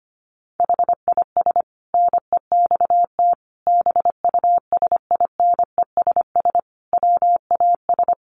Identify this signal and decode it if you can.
Morse code, heard as a keyed tone, about 25 wpm, 700 Hz. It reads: HIS NEXT BUSINESS WAS
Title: Sentences from Top 1300 Words 25wpm